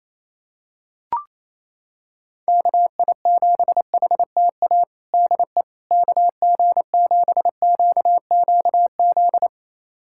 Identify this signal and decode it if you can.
Morse code, heard as a keyed tone, about 27 wpm, 700 Hz. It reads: KI7HTA DE KG7QQZ
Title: USA Callsign DE Callsign - 6 Characters - 28wpm